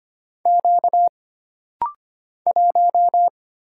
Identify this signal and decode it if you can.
Morse code, heard as a keyed tone, about 25 wpm, 700 Hz. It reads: Q 1
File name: Single Character (Letter:Number:Punctuation:Prosign) - Rapid-Fire - 25wpm